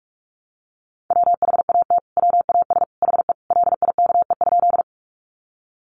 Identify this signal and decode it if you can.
Morse code, heard as a keyed tone, about 44 wpm, 700 Hz. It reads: WHAT WAS HE LIKE?
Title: Sentences from Top 100 Words 45wpm